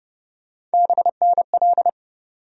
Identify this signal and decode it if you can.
Morse code, heard as a keyed tone, about 30 wpm, 700 Hz. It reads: BNL
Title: USA Call Sign Postfixes - 3 Letters 30wpm